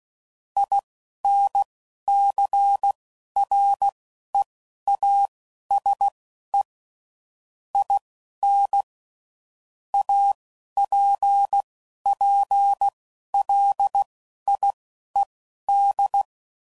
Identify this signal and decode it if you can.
Morse code, heard as a keyed tone, about 16 wpm, spaced as for 11 wpm, 800 Hz. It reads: INCREASE IN APPLIED
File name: CW Code Groups @10WPM / Technical Subjects / XC